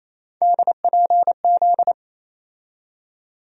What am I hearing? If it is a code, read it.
Morse code, heard as a keyed tone, about 28 wpm, 700 Hz. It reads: DPZ
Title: USA Call Sign Postfixes - 3 Letters 28wpm